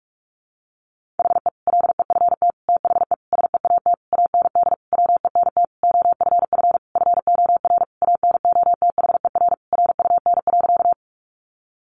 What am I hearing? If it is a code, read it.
Morse code, heard as a keyed tone, about 44 wpm, 700 Hz. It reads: HE LEFT THE SEAT AND WENT OFF FOR ANOTHER RUN.
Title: Sentences from Top 1500 Words 45wpm